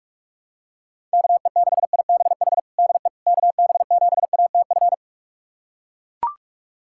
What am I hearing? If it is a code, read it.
Morse code, heard as a keyed tone, about 42 wpm, 700 Hz. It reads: KE6IBH DE KB7ATF
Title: USA Callsign DE Callsign - 6 Characters - 45wpm